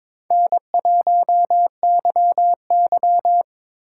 Morse code, heard as a keyed tone, about 26 wpm, 700 Hz. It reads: N1YY
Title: Speed Racing - USA 1x2 Call Signs 22wpm